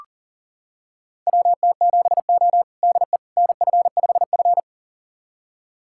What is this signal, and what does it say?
Morse code, heard as a keyed tone, about 40 wpm, 700 Hz. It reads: WT7O DE NF5F
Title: USA Callsign DE Callsign - 4 Characters - 40wpm